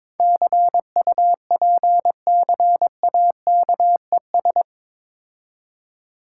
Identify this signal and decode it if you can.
Morse code, heard as a keyed tone, about 23 wpm, 690 Hz. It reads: CUPCAKES
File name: Difficult Words with Hints - 22wpm